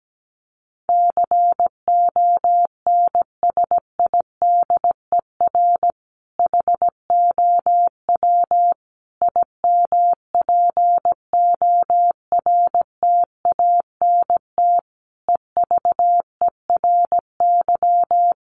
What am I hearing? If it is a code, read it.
Morse code, heard as a keyed tone, about 17 wpm, 700 Hz. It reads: CONSIDER HOW IMPORTANT EVERY
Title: Sentences from Top 1000 Words 17wpm